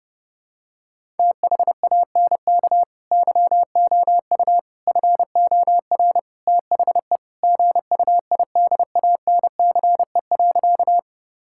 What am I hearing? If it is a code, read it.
Morse code, heard as a keyed tone, about 30 wpm, 700 Hz. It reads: THANK YOU FOR THE GUIDANCE.
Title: Sentences from Top 1600 Words 30wpm